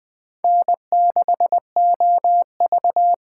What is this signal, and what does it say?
Morse code, heard as a keyed tone, about 20 wpm, 700 Hz. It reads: N6OV
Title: USA Call Signs - 5 Characters - 8x Character Spacing - 20wpm